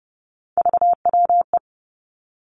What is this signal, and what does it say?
Morse code, heard as a keyed tone, about 30 wpm, 700 Hz. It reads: VWE
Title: Three Challenging Letter-Number Combinations 30wpm